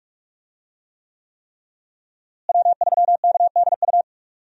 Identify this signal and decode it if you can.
Morse code, heard as a keyed tone, about 46 wpm, 700 Hz. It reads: W3KDU
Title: USA Call Signs - 5 Characters - Repeated 3x - 45wpm